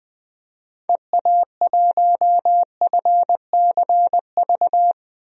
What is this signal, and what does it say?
Morse code, heard as a keyed tone, about 21 wpm, 700 Hz. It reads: EA1FCV
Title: DXCC-100 Call Signs - 6 Characters - Repeated 2x - 20wpm